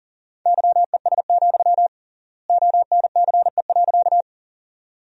Morse code, heard as a keyed tone, about 40 wpm, 700 Hz. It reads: YES, ONCE.
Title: 3x Word-Spacing - Sentences from Top 500 Words - 40wpm